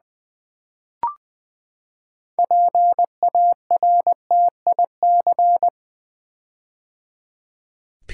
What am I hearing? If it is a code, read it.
Morse code, heard as a keyed tone, about 20 wpm, 700 Hz. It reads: PARTIC
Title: Build A Word - 6 Letters At A Time - 12-Letter Words - 20wpm